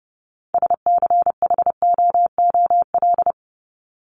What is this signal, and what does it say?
Morse code, heard as a keyed tone, about 30 wpm, 700 Hz. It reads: SCHOOL
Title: Top 300 Words 30wpm